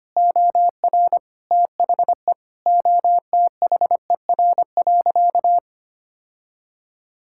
Morse code, heard as a keyed tone, about 25 wpm, 700 Hz. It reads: OR THE OTHER.
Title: Sentences from Top 1000 Words - Review Part 6 - 25wpm